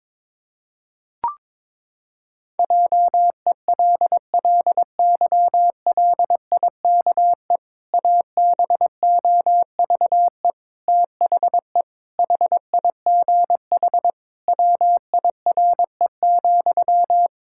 Morse code, heard as a keyed tone, about 22 wpm, 700 Hz. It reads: JELLYLIKE ABOVE THE HIGH WIRE,
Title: Pangram Challenge v2 - 22wpm